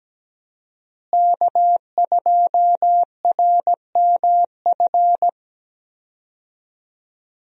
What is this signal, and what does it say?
Morse code, heard as a keyed tone, about 17 wpm, 700 Hz. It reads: K2RMF
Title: USA Call Signs - 5 Characters - Repeated 2x - 17wpm